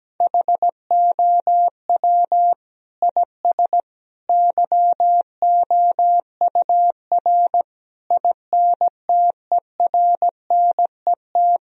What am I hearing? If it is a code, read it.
Morse code, heard as a keyed tone, about 17 wpm, 700 Hz. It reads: HOW IS YOUR INTERNET
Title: Sentences from Top 1000 Words - Review Part 4 - 17wpm